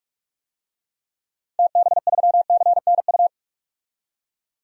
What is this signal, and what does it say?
Morse code, heard as a keyed tone, about 46 wpm, 700 Hz. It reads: TB3CNU
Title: DXCC-100 Call Signs - 6 Characters - 45wpm